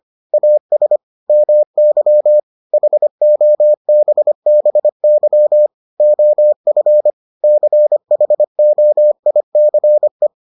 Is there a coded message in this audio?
Yes: Morse code from 0.3 to 10.3 s, heard as a keyed tone, about 25 wpm, 600 Hz. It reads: AS MY HOBBY OF CHOICE